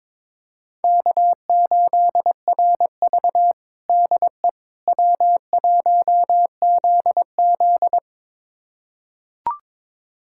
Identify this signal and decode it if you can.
Morse code, heard as a keyed tone, about 22 wpm, 700 Hz. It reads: K8RV DE W1ZZ
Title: USA Callsign DE Callsign - 4 Characters - 22wpm